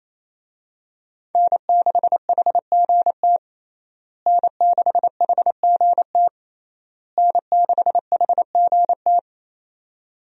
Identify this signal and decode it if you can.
Morse code, heard as a keyed tone, about 28 wpm, 700 Hz. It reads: N6HGT N6HGT N6HGT
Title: USA Call Signs - 5 Characters - Repeated 3x - 28wpm